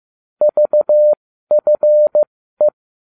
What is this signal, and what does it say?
Morse code, heard as a keyed tone, about 15 wpm, spaced as for 12 wpm, 600 Hz. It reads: VFE